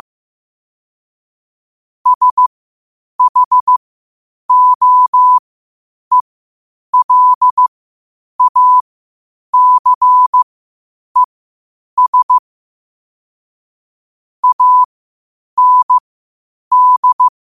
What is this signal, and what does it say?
Morse code, heard as a keyed tone, about 15 wpm, spaced as for 8 wpm, 1000 Hz. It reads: SHOELACES AND
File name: qotdfiles / 8wpm